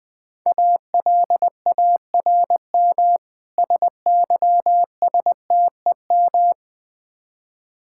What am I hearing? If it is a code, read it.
Morse code, heard as a keyed tone, about 20 wpm, 700 Hz. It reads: ALARM SYSTEM
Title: Easy Words - Speed-Racing with Hints - 20wpm